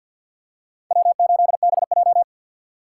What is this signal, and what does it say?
Morse code, heard as a keyed tone, about 51 wpm, 700 Hz. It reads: W8BJ
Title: USA Call Signs - 4 Characters - 6x Character Spacing - 50wpm